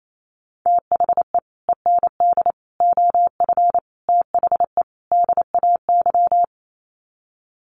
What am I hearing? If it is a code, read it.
Morse code, heard as a keyed tone, about 28 wpm, 700 Hz. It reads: THE END OF THE DAY
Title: Sets of 5 Words 28wpm